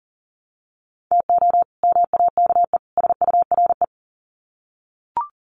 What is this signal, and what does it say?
Morse code, heard as a keyed tone, about 40 wpm, 700 Hz. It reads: TO MAKE SURE
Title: Sets of 3 Words 40wpm